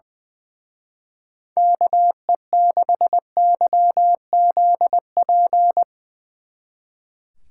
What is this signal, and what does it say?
Morse code, heard as a keyed tone, about 21 wpm, 700 Hz. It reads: KE6YZP
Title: Speed Racing - USA 2x3 Call Signs 20wpm